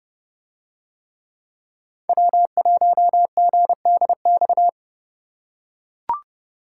Morse code, heard as a keyed tone, about 29 wpm, 700 Hz. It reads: W1GDX